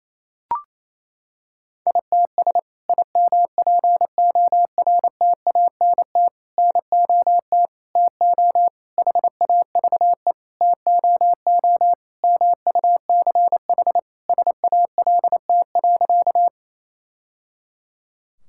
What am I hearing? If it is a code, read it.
Morse code, heard as a keyed tone, about 28 wpm, 700 Hz. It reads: ITS IMPORTANT NOT TO HAVE TOO MUCH SALT.
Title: Sentences from Top 1000 Words - Review Part 5 - 28wpm